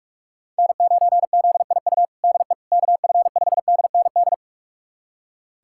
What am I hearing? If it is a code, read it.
Morse code, heard as a keyed tone, about 41 wpm, 700 Hz. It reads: N9ZIU DE KF5DND
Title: USA Callsign DE Callsign 45wpm